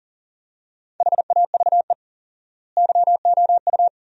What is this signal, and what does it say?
Morse code, heard as a keyed tone, about 40 wpm, 700 Hz. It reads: HAVE YOU
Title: 4x Word-Spacing - Sentences from Top 300 Words - 40wpm